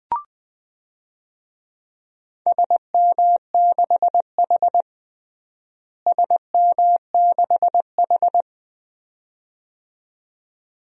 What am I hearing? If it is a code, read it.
Morse code, heard as a keyed tone, about 21 wpm, 700 Hz. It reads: SM6H SM6H
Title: DXCC-100 Call Signs - 4 Characters - Repeated 2x - 20wpm